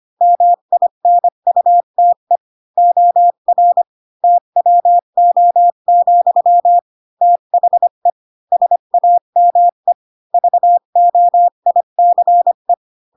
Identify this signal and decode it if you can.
Morse code, heard as a keyed tone, about 25 wpm, 700 Hz. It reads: MINUTE OR TWO, THE SAME VOICE